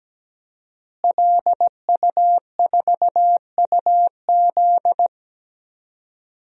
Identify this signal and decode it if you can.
Morse code, heard as a keyed tone, about 17 wpm, 700 Hz. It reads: LU4UZ
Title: DXCC-100 Call Signs - 5 Characters - Repeated 2x - 17wpm